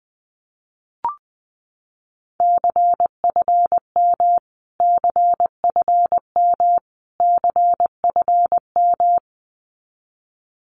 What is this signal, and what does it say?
Morse code, heard as a keyed tone, about 20 wpm, 700 Hz. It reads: CFM CFM CFM
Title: Abbreviations Repeated 3x - 20wpm